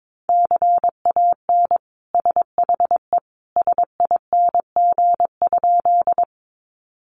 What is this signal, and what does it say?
Morse code, heard as a keyed tone, about 22 wpm, 700 Hz. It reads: CAN SHE SING?
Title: Sentences from Top 400 Words 22wpm